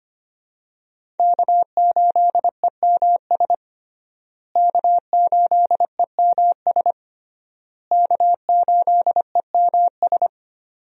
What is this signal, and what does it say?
Morse code, heard as a keyed tone, about 25 wpm, 700 Hz. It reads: K8EMS K8EMS K8EMS
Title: USA Call Signs - 5 Characters - Repeated 3x - 25wpm